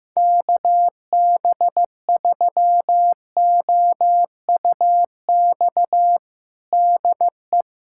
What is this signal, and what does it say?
Morse code, heard as a keyed tone, about 15 wpm, 700 Hz. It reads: KB3OUX DE
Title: USA Callsign DE Callsign - 6 Characters - 15wpm